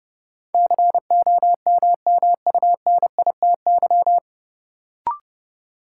Morse code, heard as a keyed tone, about 30 wpm, 700 Hz. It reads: COMMUNITY
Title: Sentences from Top 1000 Words - Review Part 4 - New Word Drill - 30wpm